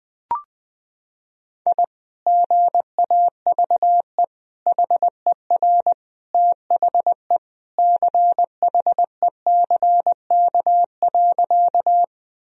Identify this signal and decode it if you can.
Morse code, heard as a keyed tone, about 19 wpm, 700 Hz. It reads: I GAVE HER THE CHECK.
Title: Sentences from Top 1000 Words - Review Part 3 - 20wpm